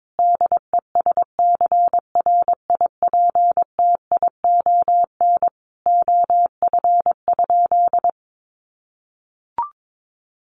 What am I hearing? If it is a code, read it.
Morse code, heard as a keyed tone, about 22 wpm, 700 Hz. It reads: DESCRIPTION OF?
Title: Sentences from Top 1300 Words 22wpm